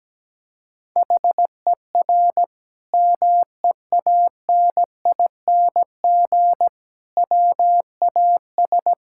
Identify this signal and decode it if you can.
Morse code, heard as a keyed tone, about 17 wpm, 700 Hz. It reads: HER MEANING WAS
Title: Sentences from Top 1000 Words - Review Part 6 - 17wpm